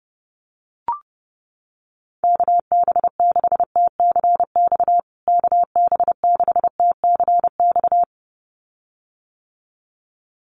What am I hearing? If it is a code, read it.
Morse code, heard as a keyed tone, about 30 wpm, 700 Hz. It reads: KB6TCX KB6TCX
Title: USA Call Signs Repeated 2x 30wpm